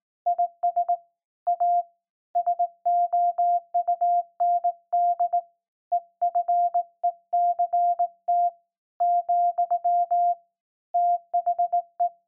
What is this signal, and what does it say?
Morse code, heard as a keyed tone, about 19 wpm, 700 Hz. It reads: IS A SOUND EFECT , THE
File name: se / pre2 / Morse Code01-1(Fast)